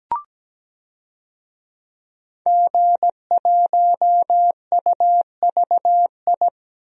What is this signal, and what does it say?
Morse code, heard as a keyed tone, about 17 wpm, 700 Hz. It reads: G1UVI
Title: DXCC-100 Call Signs - 5 Characters - 17wpm